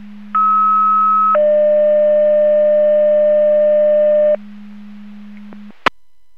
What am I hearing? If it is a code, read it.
Two-tone page, heard as a one-second tone, then a three-second tone, about 1290 then 600 Hz. It is heard from 0.3 to 4.4 s.